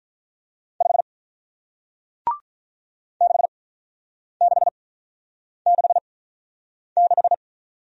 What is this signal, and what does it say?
Morse code, heard as a keyed tone, about 47 wpm, spaced as for 9 wpm, 700 Hz. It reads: 5 6666